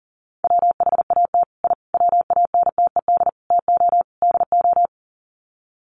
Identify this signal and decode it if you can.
Morse code, heard as a keyed tone, about 40 wpm, 700 Hz. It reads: WHAT I WANTED TO DO
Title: Sets of 5 Words Encore2 40wpm